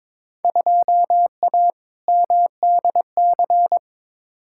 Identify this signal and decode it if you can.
Morse code, heard as a keyed tone, about 21 wpm, 700 Hz. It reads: 2A MDC
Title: Field Day Exchanges Encore - 22wpm